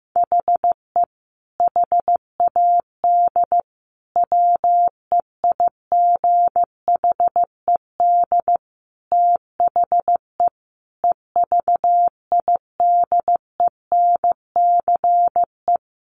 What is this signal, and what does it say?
Morse code, heard as a keyed tone, about 15 wpm, 700 Hz. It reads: HE HAD WEIGHED THE EVIDENCE
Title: A Sound Of Thunder 15wpm